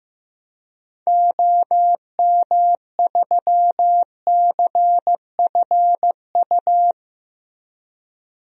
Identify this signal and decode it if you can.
Morse code, heard as a keyed tone, about 15 wpm, 700 Hz. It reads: OM3CFU
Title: DXCC-100 Call Signs - 6 Characters - Repeated 3x - 15wpm